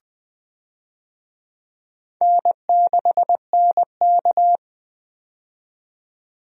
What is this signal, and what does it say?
Morse code, heard as a keyed tone, about 20 wpm, 700 Hz. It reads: N6NK
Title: USA Call Signs - 4 Characters - 20wpm